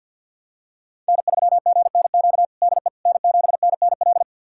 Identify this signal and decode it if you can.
Morse code, heard as a keyed tone, about 52 wpm, 700 Hz. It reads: N3GNQ DE N7NNL
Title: USA Callsign DE Callsign - 5 Characters - 50wpm